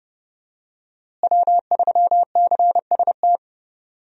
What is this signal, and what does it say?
Morse code, heard as a keyed tone, about 31 wpm, 700 Hz. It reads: W3CST
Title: USA Call Signs - 5 Characters - Repeated 2x - 30wpm